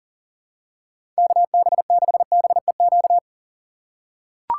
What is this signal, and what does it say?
Morse code, heard as a keyed tone, about 40 wpm, 700 Hz. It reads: KB6BEQ
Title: USA Call Signs - 2x Character Spacing - 40wpm